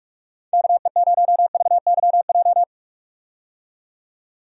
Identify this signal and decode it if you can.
Morse code, heard as a keyed tone, about 41 wpm, 700 Hz. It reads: KE0VYJ
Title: USA Call Signs - 6 Characters - 45wpm